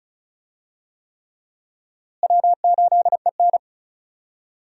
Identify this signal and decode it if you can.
Morse code, heard as a keyed tone, about 36 wpm, 700 Hz. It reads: W8EN
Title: USA Call Signs - 4 Characters - Repeated 3x - 35wpm